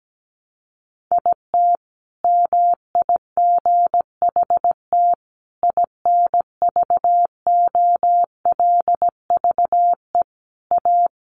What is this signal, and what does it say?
Morse code, heard as a keyed tone, about 17 wpm, 700 Hz. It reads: IT MIGHT INVOLVE A
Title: Sentences from Top 1200 Words 17wpm